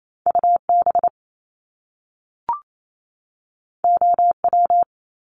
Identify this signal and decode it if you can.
Morse code, heard as a keyed tone, about 28 wpm, 700 Hz. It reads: UB OW
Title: Top Two Letter Combinations in Words 28wpm